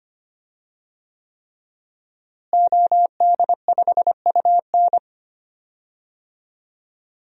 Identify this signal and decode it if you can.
Morse code, heard as a keyed tone, about 25 wpm, 700 Hz. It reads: OD5UN
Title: DXCC-100 Call Signs - 5 Characters - Repeated 3x - 25wpm